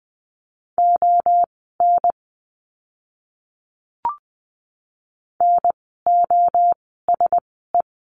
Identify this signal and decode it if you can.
Morse code, heard as a keyed tone, about 20 wpm, spaced as for 14 wpm, 700 Hz. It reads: ON NOSE
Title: 036 Lesson 7f - Letter s - Words 20wpm